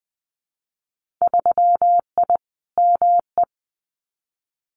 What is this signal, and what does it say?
Morse code, heard as a keyed tone, about 20 wpm, 700 Hz. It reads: 3I ME
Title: Winter Field Day Exchanges 20wpm